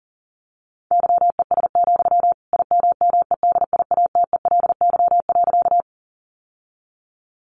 Morse code, heard as a keyed tone, about 40 wpm, 700 Hz. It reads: YES, IMMEDIATELY.
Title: Sentences from Top 1100 Words 40wpm